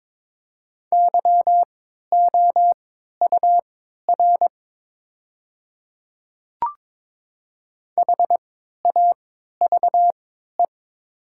Morse code, heard as a keyed tone, about 22 wpm, spaced as for 13 wpm, 700 Hz. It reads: YOUR HAVE
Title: 3x Character Spacing - Top 100 Words - 22wpm